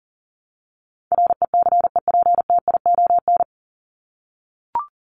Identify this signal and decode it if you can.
Morse code, heard as a keyed tone, about 37 wpm, 700 Hz. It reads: RECEPTION
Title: Top 1751-2000 Words 40wpm